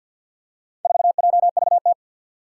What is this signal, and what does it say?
Morse code, heard as a keyed tone, about 50 wpm, 700 Hz. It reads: 4JVT
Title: Four Challenging Letter-Number Combinations 50wpm